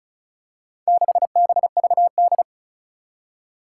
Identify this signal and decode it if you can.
Morse code, heard as a keyed tone, about 36 wpm, 700 Hz. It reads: 6BVD
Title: Four Challenging Letter-Number Combinations 35wpm